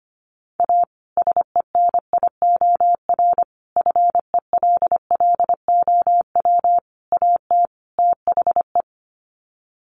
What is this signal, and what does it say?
Morse code, heard as a keyed tone, about 25 wpm, 700 Hz. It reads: A SENIOR FELLOW AT THE
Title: Sets of 5 Words Encore 25wpm